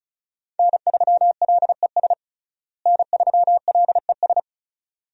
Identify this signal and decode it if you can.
Morse code, heard as a keyed tone, about 34 wpm, 700 Hz. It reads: N3LES N3LES
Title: USA Call Signs - 5 Characters - Repeated 2x - 35wpm